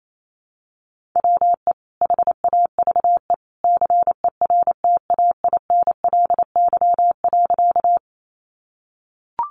Morse code, heard as a keyed tone, about 28 wpm, 690 Hz. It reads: WE HAVE CERTAINLY.
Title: Sentences from Top 900 Words 28wpm